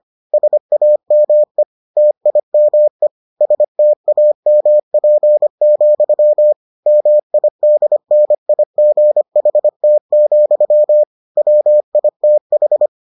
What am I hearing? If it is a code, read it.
Morse code, heard as a keyed tone, about 25 wpm, 600 Hz. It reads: SAME TIME STAMP, MIDNIGHT, WITH